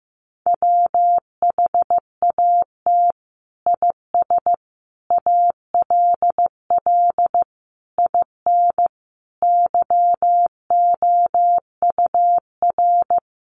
Morse code, heard as a keyed tone, about 15 wpm, 700 Hz. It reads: WHAT IS ALL IN YOUR
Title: Sentences from Top 1200 Words 15wpm